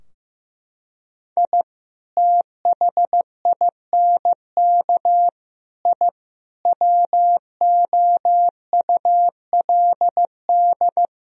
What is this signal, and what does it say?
Morse code, heard as a keyed tone, about 15 wpm, 700 Hz. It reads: I THINK I WOULD
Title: Sentences from Top 1000 Words - Review Part 4 - 15wpm